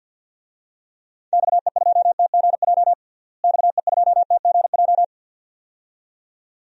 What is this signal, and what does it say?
Morse code, heard as a keyed tone, about 53 wpm, 700 Hz. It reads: XE2TGJ XE2TGJ
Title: DXCC-100 Call Signs - 6 Characters - Repeated 2x - 50wpm